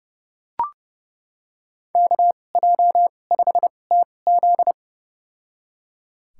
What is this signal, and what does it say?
Morse code, heard as a keyed tone, about 30 wpm, spaced as for 22 wpm, 700 Hz. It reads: KJ5TZ
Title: USA Call Signs - 5 Characters - 2x Character Spacing - 30wpm